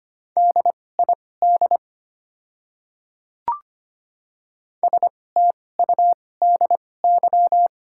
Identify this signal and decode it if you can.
Morse code, heard as a keyed tone, about 25 wpm, spaced as for 18 wpm, 690 Hz. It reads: DID STUDY
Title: 2x Character Spacing - Top 300 Words - 25wpm